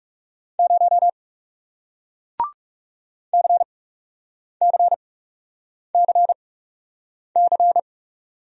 Morse code, heard as a keyed tone, about 41 wpm, spaced as for 8 wpm, 700 Hz. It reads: 0 CCCC